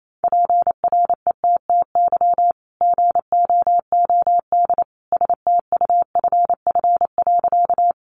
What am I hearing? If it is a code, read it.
Morse code, heard as a keyed tone, about 28 wpm, 700 Hz. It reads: PRETTY GOOD STUFF.
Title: Sentences from Top 1400 Words 28wpm